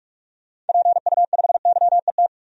Morse code, heard as a keyed tone, about 46 wpm, 700 Hz. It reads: PU5YET
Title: DXCC-100 Call Signs - 6 Characters - 45wpm